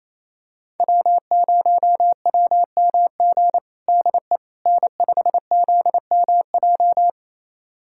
Morse code, heard as a keyed tone, about 28 wpm, 700 Hz. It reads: W0WMG DE N5ZMJ
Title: USA Callsign DE Callsign - 5 Characters - 28wpm